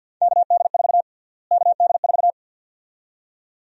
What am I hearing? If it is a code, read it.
Morse code, heard as a keyed tone, about 51 wpm, 700 Hz. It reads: KD4 KD4
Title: USA Call Sign Prefixes - Repeated 2x - 50wpm